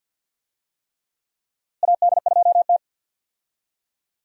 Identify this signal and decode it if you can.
Morse code, heard as a keyed tone, about 48 wpm, 700 Hz. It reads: AD2T
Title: USA Call Signs - 4 Characters - Repeated 3x - 50wpm